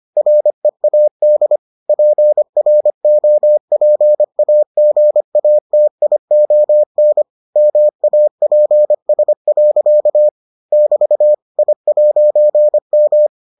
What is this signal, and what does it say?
Morse code, heard as a keyed tone, about 25 wpm, 600 Hz. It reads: READ PROPAGATION MAPS. <BT> I'M